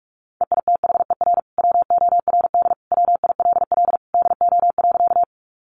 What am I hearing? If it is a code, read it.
Morse code, heard as a keyed tone, about 44 wpm, 700 Hz. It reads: EITHER WORD WILL DO.
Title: Sentences from Top 600 Words 45wpm